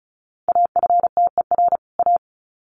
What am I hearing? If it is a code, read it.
Morse code, heard as a keyed tone, about 34 wpm, 700 Hz. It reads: AFTER A
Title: Sets of 2 Words 35wpm